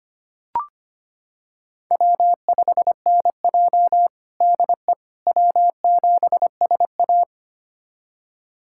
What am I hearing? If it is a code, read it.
Morse code, heard as a keyed tone, about 25 wpm, 700 Hz. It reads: W5NJ DE W7SA
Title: USA Callsign DE Callsign - 4 Characters - 25wpm